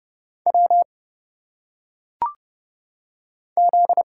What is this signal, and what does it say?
Morse code, heard as a keyed tone, about 30 wpm, 700 Hz. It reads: W Z
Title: Single Letters - Phonetic Pronunciation 30wpm